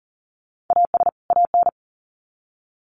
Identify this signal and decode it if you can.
Morse code, heard as a keyed tone, about 40 wpm, 700 Hz. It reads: AS AN
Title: Sets of 2 Words 40wpm